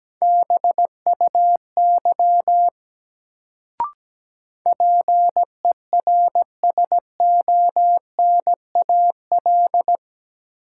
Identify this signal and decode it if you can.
Morse code, heard as a keyed tone, about 17 wpm, 700 Hz. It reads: BUY PERSONAL
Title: Sentences from Top 700 Words - New Word Drill - 17wpm